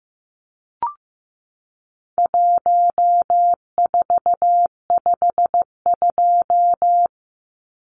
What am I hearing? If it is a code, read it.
Morse code, heard as a keyed tone, about 15 wpm, 700 Hz. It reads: 1452
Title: Numbers - Contest Style - 4-Digit 15wpm